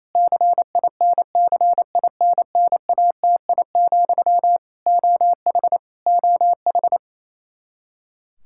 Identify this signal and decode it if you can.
Morse code, heard as a keyed tone, about 28 wpm, 700 Hz. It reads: CINCINNATI, OH OH
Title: Practice QSO - Exchange 1 - Beginning - 28wpm